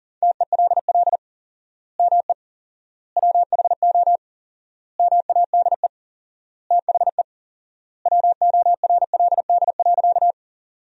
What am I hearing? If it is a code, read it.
Morse code, heard as a keyed tone, about 40 wpm, 700 Hz. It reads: TELL ME WHO MADE THE WORLD.
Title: 4x Word-Spacing - Sentences from Top 200 Words - 40wpm